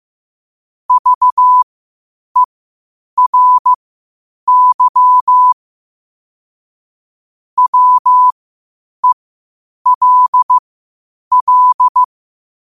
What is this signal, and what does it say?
Morse code, heard as a keyed tone, about 15 wpm, spaced as for 8 wpm, 1000 Hz. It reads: VERY WELL
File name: qotdfiles / 8wpm